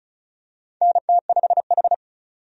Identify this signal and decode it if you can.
Morse code, heard as a keyed tone, about 34 wpm, 700 Hz. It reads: NT5H
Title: USA Call Signs - 4 Characters - 35wpm